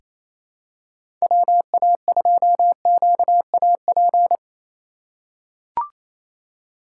Morse code, heard as a keyed tone, about 27 wpm, 700 Hz. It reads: WA2QAP
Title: USA Call Signs - 6 Characters - 8x Character Spacing - 28wpm